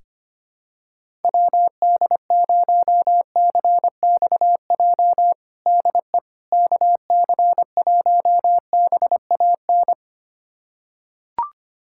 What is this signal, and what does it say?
Morse code, heard as a keyed tone, about 25 wpm, 700 Hz. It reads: WD0CXJ DE KC1BAN
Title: USA Callsign DE Callsign - 6 Characters - 25wpm